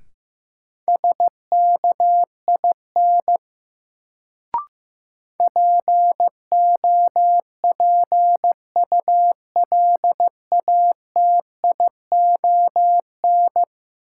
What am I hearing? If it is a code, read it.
Morse code, heard as a keyed tone, about 15 wpm, 700 Hz. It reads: SKIN POPULATION
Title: Sentences from Top 1300 Words - New Word Drill Alt - 15wpm